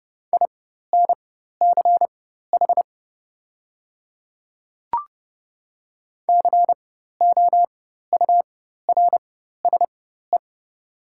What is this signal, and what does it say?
Morse code, heard as a keyed tone, about 30 wpm, spaced as for 14 wpm, 700 Hz. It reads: INCH COURSE
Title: 4x Character Spacing - Top 500 Words - 30wpm